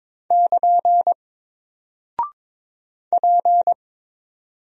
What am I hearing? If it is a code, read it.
Morse code, heard as a keyed tone, about 21 wpm, 700 Hz. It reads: <KN> P
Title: Single Character (Letter:Number:Punctuation:Prosign) Alt 22wpm